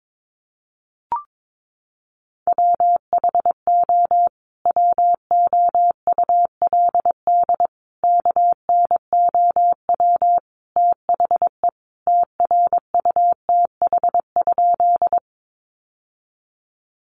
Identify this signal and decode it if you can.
Morse code, heard as a keyed tone, about 22 wpm, 700 Hz. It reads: WHO WOULD KNOW THE TRUTH?
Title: Sentences from Top 800 Words 22wpm